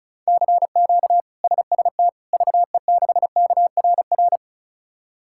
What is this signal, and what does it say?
Morse code, heard as a keyed tone, about 34 wpm, 700 Hz. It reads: CQ SST VE6KRR
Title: K1USN SST QSO Practice - 35wpm